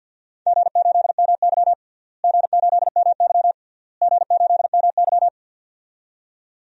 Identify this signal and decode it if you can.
Morse code, heard as a keyed tone, about 49 wpm, 700 Hz. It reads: G8MY G8MY G8MY
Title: DXCC-100 Call Signs - 4 Characters - Repeated 3x - 50wpm